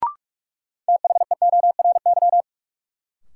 Morse code, heard as a keyed tone, about 42 wpm, 700 Hz. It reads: THEORY
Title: Sentences from Top 600 Words - New Word Drill - 45wpm